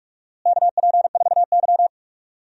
Morse code, heard as a keyed tone, about 46 wpm, 700 Hz. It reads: KP4Y
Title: DXCC-100 Call Signs - 4 Characters - Repeated 3x - 45wpm